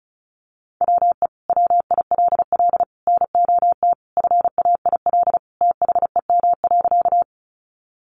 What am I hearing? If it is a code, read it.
Morse code, heard as a keyed tone, about 34 wpm, 700 Hz. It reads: WE WILL NOT FAIL THEM.
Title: Sentences from Top 1100 Words 35wpm